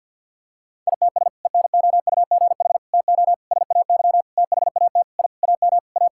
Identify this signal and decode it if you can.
Morse code, heard as a keyed tone, about 51 wpm, 700 Hz. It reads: ITS ENOUGH TO SAY THAT I AM A
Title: Sentences from Top 1000 Words - Review Part 6 - 50wpm